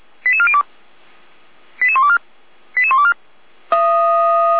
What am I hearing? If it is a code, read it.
Five-tone selective call (ZVEI), heard as quick runs of five tones, tones of about 70 ms. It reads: 88402 88124 88124
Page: SDR/ZVEI